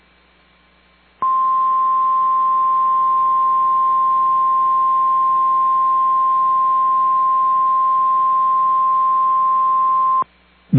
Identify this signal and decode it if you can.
Emergency Alert System (SAME) tones (the 1050 Hz attention tone). It is heard from 1.2 to 10.3 s.
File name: EAS Logs / KMNR